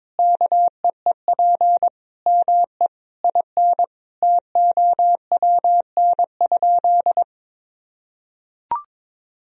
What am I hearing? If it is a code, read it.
Morse code, heard as a keyed tone, about 22 wpm, 700 Hz. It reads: KEEP ME IN TOWN?
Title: Sentences from Top 500 Words and Abbreviations 22wpm